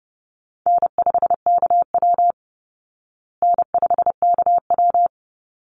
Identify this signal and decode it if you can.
Morse code, heard as a keyed tone, about 28 wpm, 700 Hz. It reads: N5KW N5KW
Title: Call Signs Repeated 3x - 30wpm